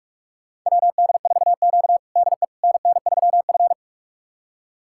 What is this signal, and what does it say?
Morse code, heard as a keyed tone, about 46 wpm, 700 Hz. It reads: WD4Q DE NN3F
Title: USA Callsign DE Callsign - 4 Characters - 45wpm